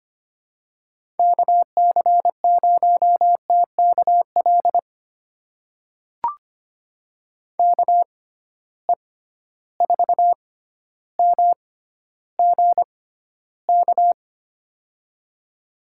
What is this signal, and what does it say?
Morse code, heard as a keyed tone, about 25 wpm, spaced as for 13 wpm, 700 Hz. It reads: KC0TKL KE4MGK
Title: USA Call Signs - 6 Characters - 6x Character Spacing - 25wpm